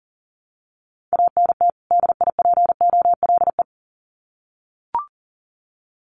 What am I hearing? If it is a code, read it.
Morse code, heard as a keyed tone, about 40 wpm, 700 Hz. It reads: ANT DIPOLE
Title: Common QSO Phrases 40wpm